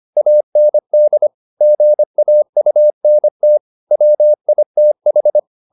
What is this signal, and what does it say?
Morse code, heard as a keyed tone, about 25 wpm, 600 Hz. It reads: AND GAUNT WITH
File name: the old man and the sea part 1 25wpm